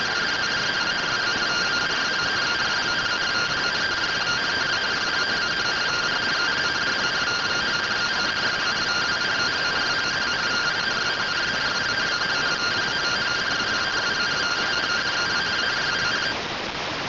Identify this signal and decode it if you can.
Two-tone teleprinter signal, about 170 Hz shift.